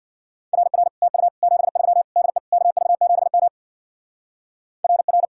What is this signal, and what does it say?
Morse code, heard as a keyed tone, about 62 wpm, 700 Hz. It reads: RR TU 73 DE KV7M RR
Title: Speed-Racing Common QSO Phrases Encore - 50wpm